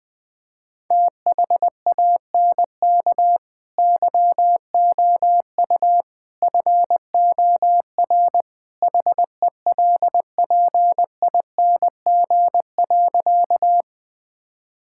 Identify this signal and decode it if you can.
Morse code, heard as a keyed tone, about 20 wpm, 700 Hz. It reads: THANK YOU FOR HELPING.
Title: Sentences with 3-Word Phrases - 20wpm